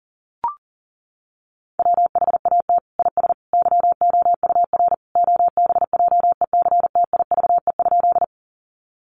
Morse code, heard as a keyed tone, about 40 wpm, 700 Hz. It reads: WHAT IS YOUR OBJECTIVE?
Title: Sentences from Top 1200 Words 40wpm